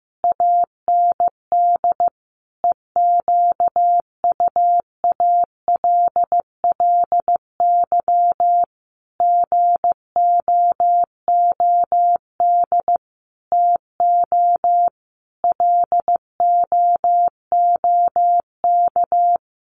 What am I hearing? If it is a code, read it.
Morse code, heard as a keyed tone, about 15 wpm, 700 Hz. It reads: AND EQUALLY GOOD TO LOOK
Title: Sentences from Top 1400 Words 15wpm